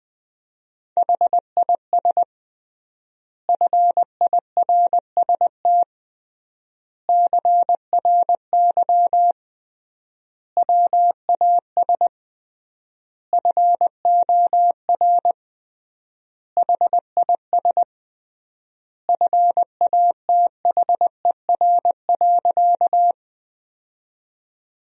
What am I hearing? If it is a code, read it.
Morse code, heard as a keyed tone, about 20 wpm, 700 Hz. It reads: HIS FIRST CRY WAS FOR HIS FATHER.
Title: 3x Word-Spacing - Sentences from Top 500 Words - 20wpm